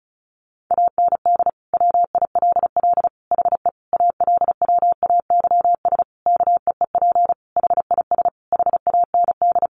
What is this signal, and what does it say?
Morse code, heard as a keyed tone, about 34 wpm, 700 Hz. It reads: AND WILL HE ALWAYS KEEP HIS HAND
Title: Sentences from Top 1200 Words 35wpm